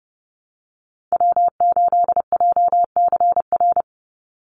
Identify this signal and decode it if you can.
Morse code, heard as a keyed tone, about 30 wpm, 700 Hz. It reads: W8JCR
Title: USA Call Signs 30wpm